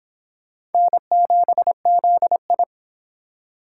Morse code, heard as a keyed tone, about 29 wpm, 700 Hz. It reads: N7ZI